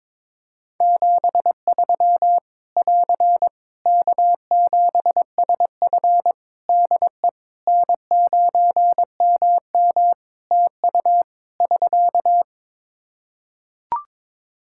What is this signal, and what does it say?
Morse code, heard as a keyed tone, about 22 wpm, 700 Hz. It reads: 73 <AR> K7SF DE N9MM TU <SK>
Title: K1USN SST QSO Practice 22wpm